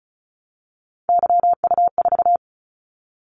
Morse code, heard as a keyed tone, about 36 wpm, 700 Hz. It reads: YU4
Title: Callsign Prefixes for Top 25 DXCC - 35wpm